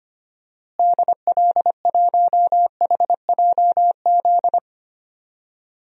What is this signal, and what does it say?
Morse code, heard as a keyed tone, about 25 wpm, 700 Hz. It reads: DL1HJZ
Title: DXCC-100 Call Signs - 6 Characters - Repeated 3x - 25wpm